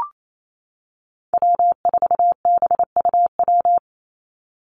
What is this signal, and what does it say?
Morse code, heard as a keyed tone, about 28 wpm, 700 Hz. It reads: W4BUW
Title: Call Signs - 28wpm